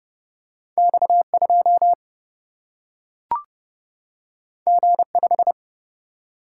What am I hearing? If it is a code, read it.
Morse code, heard as a keyed tone, about 30 wpm, 700 Hz. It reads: X 2 G 5
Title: Single-Letter Single-Number 30wpm